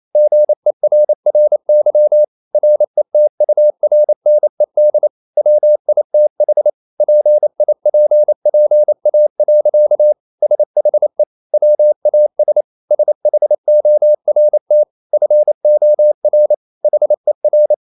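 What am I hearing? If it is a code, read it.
Morse code, heard as a keyed tone, about 28 wpm, 600 Hz. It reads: GERRY RETURNED WITH PIPPA. SHE WAS SHORT FOR HER
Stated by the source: Morse code practice at 28 wpm, text of Ken Follett's novel Never in English, from CW med Gnister.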